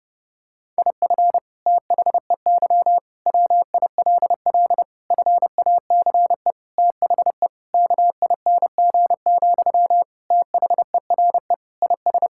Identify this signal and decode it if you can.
Morse code, heard as a keyed tone, about 31 wpm, 700 Hz. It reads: IF THEY WILL FACE THE KING, THERE IS
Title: Sentences from Top 1000 Words - Review Part 6 - 30wpm